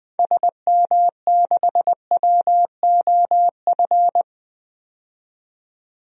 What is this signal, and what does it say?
Morse code, heard as a keyed tone, about 20 wpm, 700 Hz. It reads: SM6WOF
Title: DXCC-100 Call Signs - 6 Characters - 20wpm